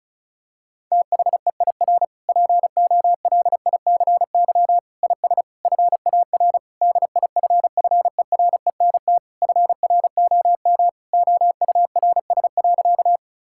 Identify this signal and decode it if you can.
Morse code, heard as a keyed tone, about 34 wpm, 700 Hz. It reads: THEIR POLICY IS FAR DIFFERENT FROM OURS.
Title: Sentences from Top 1000 Words - Review Part 5 - 35wpm